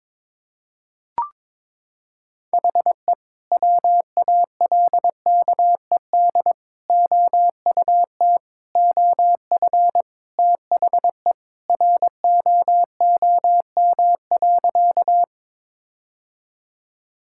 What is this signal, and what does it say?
Morse code, heard as a keyed tone, about 22 wpm, 700 Hz. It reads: HE WALKED OUT OF THE ROOM.
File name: practice / 22wpm / Sentences with 2-Word Phrases - 22wpm